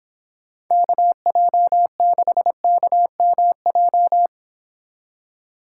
Morse code, heard as a keyed tone, about 29 wpm, 700 Hz. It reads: KJ6KMJ